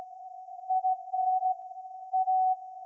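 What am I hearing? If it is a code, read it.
Morse code, heard as a keyed tone, about 16 wpm, 750 Hz. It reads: IN A
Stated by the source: Morse code from SAQ, received at a station in Putignano, Apulia, Italy, on VLF.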